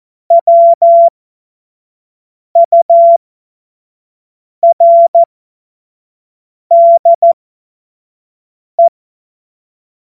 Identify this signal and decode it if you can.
Morse code, heard as a keyed tone, about 14 wpm, spaced as for 5 wpm, 680 Hz. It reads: WURDE
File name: Morsekurs / Witze / Witz 04. Juli 2023 25 F70